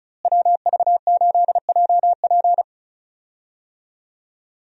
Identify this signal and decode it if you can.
Morse code, heard as a keyed tone, about 36 wpm, 700 Hz. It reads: WV8JP
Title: USA Call Signs - 5 Characters - Repeated 3x - 35wpm